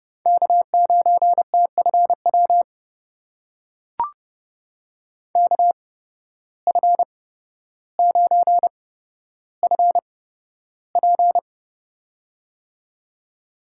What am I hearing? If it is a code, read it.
Morse code, heard as a keyed tone, about 31 wpm, spaced as for 13 wpm, 700 Hz. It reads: K9TFW KF9FP
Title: USA Call Signs - 5 Characters - 8x Character Spacing - 30wpm